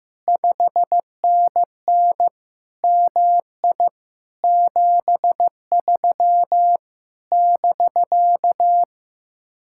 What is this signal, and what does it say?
Morse code, heard as a keyed tone, about 15 wpm, 700 Hz. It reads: 5NN MI 73 <BK>
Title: Speed-Racing Common QSO Phrases Encore - 15wpm